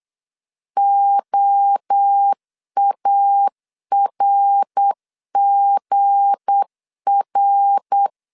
Morse code, heard as a keyed tone, about 8 wpm, 800 Hz. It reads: OARGR
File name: LEZIONI / 8 YJ45 / 8 YJ 8wpm 3313